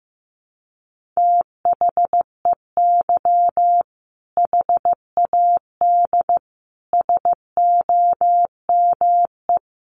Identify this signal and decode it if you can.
Morse code, heard as a keyed tone, about 15 wpm, 700 Hz. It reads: THEY HAD SOME
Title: Sentences from Top 1500 Words 15wpm